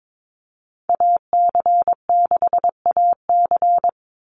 Morse code, heard as a keyed tone, about 22 wpm, 700 Hz. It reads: AC6AC
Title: Call Signs Repeated 3x - 22wpm